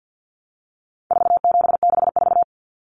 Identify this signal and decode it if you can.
Morse code, heard as a keyed tone, about 50 wpm, 700 Hz. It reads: 4764
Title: Numbers - 4-Digit 50wpm